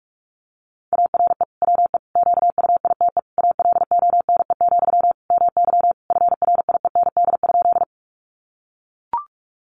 Morse code, heard as a keyed tone, about 44 wpm, 700 Hz. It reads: ARE WE QUITE ALONE, MY FRIEND?
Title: Sentences from Top 1000 Words 45wpm